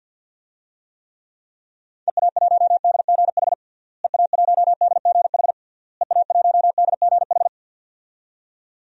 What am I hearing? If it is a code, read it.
Morse code, heard as a keyed tone, about 47 wpm, 700 Hz. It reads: EA1DGH EA1DGH EA1DGH
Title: DXCC-100 Call Signs - 6 Characters - Repeated 3x - 50wpm